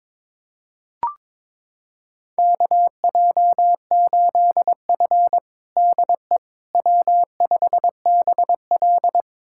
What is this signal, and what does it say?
Morse code, heard as a keyed tone, about 22 wpm, 700 Hz. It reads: KJ8F DE W5BL
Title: USA Callsign DE Callsign - 4 Characters - 22wpm